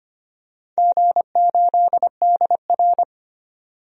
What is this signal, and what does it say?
Morse code, heard as a keyed tone, about 25 wpm, 700 Hz. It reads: G8DR
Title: DXCC-100 Call Signs - 4 Characters - 25wpm